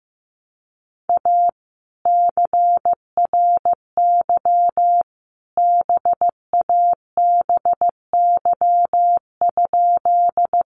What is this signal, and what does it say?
Morse code, heard as a keyed tone, about 15 wpm, 700 Hz. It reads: A CRY BABY?
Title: Sentences from Top 1300 Words 15wpm